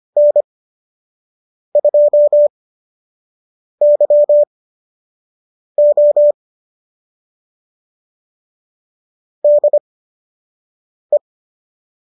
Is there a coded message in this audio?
Yes: Morse code from 0.2 to 11.2 s, heard as a keyed tone, about 25 wpm, spaced as for 6 wpm, 590 Hz. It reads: N2YO DE